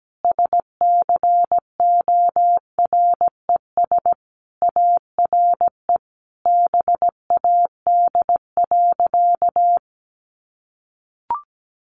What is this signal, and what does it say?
Morse code, heard as a keyed tone, about 17 wpm, 700 Hz. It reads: SCORES ARE BAD.
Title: Additional Easy Phrases 17wpm